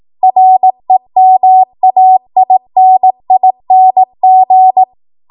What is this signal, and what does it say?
Morse code, heard as a keyed tone, about 18 wpm, 750 Hz. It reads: REMAINING
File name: Morse / 18WPM